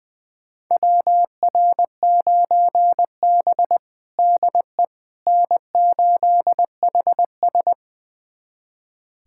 Morse code, heard as a keyed tone, about 20 wpm, 700 Hz. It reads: WR9B DE N8HS
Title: USA Callsign DE Callsign - 4 Characters - 20wpm